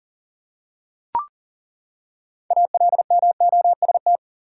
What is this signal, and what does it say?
Morse code, heard as a keyed tone, about 37 wpm, 700 Hz. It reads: ALMOST
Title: Six Letter Words - 40wpm